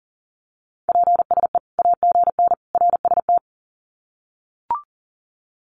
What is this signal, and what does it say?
Morse code, heard as a keyed tone, about 37 wpm, 700 Hz. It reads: PSE AGN RST
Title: Common QSO Phrases 40wpm